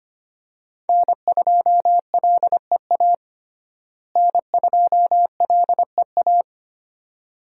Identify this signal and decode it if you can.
Morse code, heard as a keyed tone, about 25 wpm, 700 Hz. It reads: N2LEA N2LEA
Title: USA Call Signs - 5 Characters - Repeated 2x - 25wpm